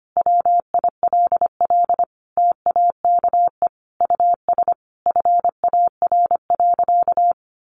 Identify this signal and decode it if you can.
Morse code, heard as a keyed tone, about 25 wpm, 700 Hz. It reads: WILL TAKE US FAR.
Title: Sentences from Top 1500 Words 25wpm